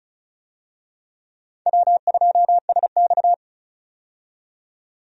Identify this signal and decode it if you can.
Morse code, heard as a keyed tone, about 34 wpm, 700 Hz. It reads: W2SX
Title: USA Call Signs - 4 Characters - 35wpm